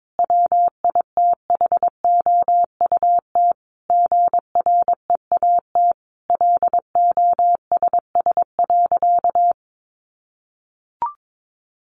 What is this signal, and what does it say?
Morse code, heard as a keyed tone, about 22 wpm, 700 Hz. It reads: WITHOUT GREAT LOSS.
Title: Sentences from Top 1400 Words 22wpm